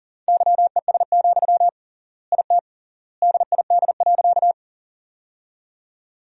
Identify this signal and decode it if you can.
Morse code, heard as a keyed tone, about 40 wpm, 700 Hz. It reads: YES, IT DID.
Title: 3x Word-Spacing - Sentences from Top 500 Words - 40wpm